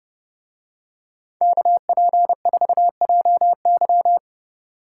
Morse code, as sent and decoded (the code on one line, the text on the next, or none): -.- .--. ....- .--- -.--
KP4JY